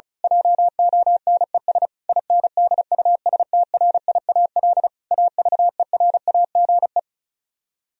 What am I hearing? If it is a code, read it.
Morse code, heard as a keyed tone, about 32 wpm, 700 Hz. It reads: JONES INDUSTRIAL AVERAGE
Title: Speed-Racing - Sets of 5 Words 35wpm